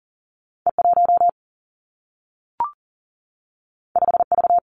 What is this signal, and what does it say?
Morse code, heard as a keyed tone, about 40 wpm, 700 Hz. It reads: E 1 5 V
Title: Two Challenging Letter-Number Combinations 40wpm